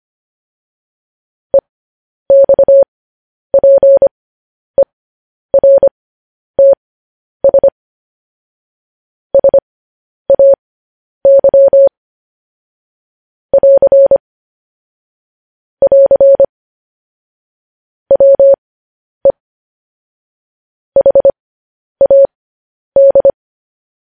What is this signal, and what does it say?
Morse code, heard as a keyed tone, about 25 wpm, spaced as for 10 wpm, 550 Hz. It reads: EXPERTS SAY <AR> <AR> WE HAD
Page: news-headlines-in-morse-code-at-10-wpm